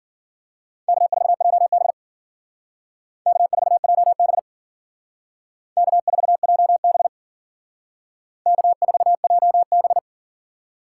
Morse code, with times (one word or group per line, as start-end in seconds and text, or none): 0.9-1.9 K4JB
3.3-4.4 K4JB
5.8-7.1 K4JB
8.5-10.0 K4JB